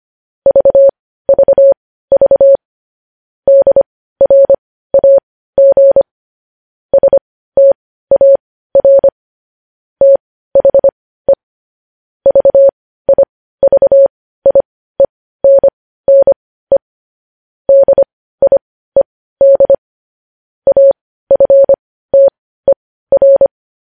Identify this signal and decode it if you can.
Morse code, heard as a keyed tone, about 25 wpm, spaced as for 15 wpm, 550 Hz. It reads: VVV DRAG STAR THE VIVIENNE DIED AFTER
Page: news headlines in morse code at 15 wpm